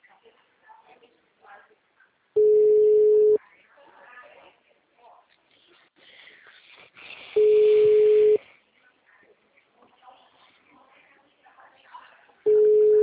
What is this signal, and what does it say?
Ringback tone (Europe, 425 Hz), with speech.